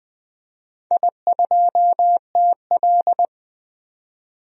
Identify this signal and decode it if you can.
Morse code, heard as a keyed tone, about 20 wpm, 700 Hz. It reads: I2TL
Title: DXCC-100 Call Signs - 4 Characters - 20wpm